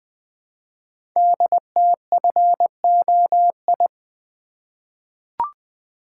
Morse code, heard as a keyed tone, about 20 wpm, 700 Hz. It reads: DTFOI
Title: Code Groups - 5 Letters - 20wpm